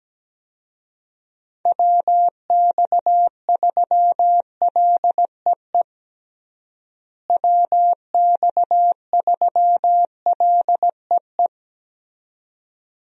Morse code, heard as a keyed tone, about 17 wpm, 700 Hz. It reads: WX3LEE WX3LEE
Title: USA Call Signs - 6 Characters - Repeated 2x - 17wpm